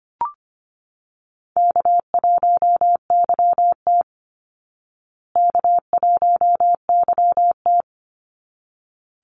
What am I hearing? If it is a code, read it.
Morse code, heard as a keyed tone, about 25 wpm, 700 Hz. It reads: K1YT K1YT
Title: Call Signs Repeated 3x - 25wpm